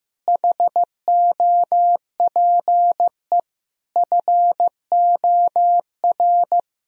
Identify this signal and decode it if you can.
Morse code, heard as a keyed tone, about 15 wpm, 700 Hz. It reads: HOPE FOR
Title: Sentences from Top 1000 Words - Review Part 4 - 15wpm